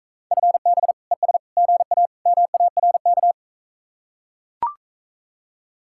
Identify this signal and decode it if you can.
Morse code, heard as a keyed tone, about 44 wpm, 700 Hz. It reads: FB ES GA MARK